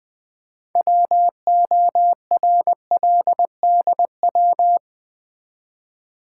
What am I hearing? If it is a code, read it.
Morse code, heard as a keyed tone, about 20 wpm, 700 Hz. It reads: WORLDW
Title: Build A Word - 3 Letters At A Time - 9-Letter Words - 20wpm